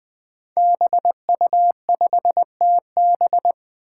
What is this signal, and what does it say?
Morse code, heard as a keyed tone, about 20 wpm, 700 Hz. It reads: BU5TB
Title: Five Challenging Letter-Number Combinations 20wpm